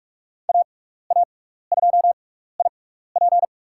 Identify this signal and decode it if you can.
Morse code, heard as a keyed tone, about 41 wpm, spaced as for 15 wpm, 690 Hz. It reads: AA2IP
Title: USA Call Signs - 5 Characters - 6x Character Spacing - 45wpm